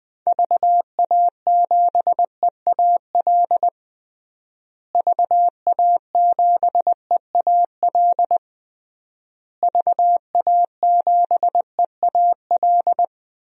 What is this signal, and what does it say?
Morse code, heard as a keyed tone, about 21 wpm, 700 Hz. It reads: VA7EAL VA7EAL VA7EAL
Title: DXCC-100 Call Signs - 6 Characters - Repeated 3x - 20wpm